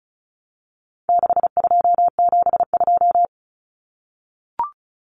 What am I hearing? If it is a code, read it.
Morse code, heard as a keyed tone, about 34 wpm, 700 Hz. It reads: 6272
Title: Common Cut Numbers - 4-Digit 35wpm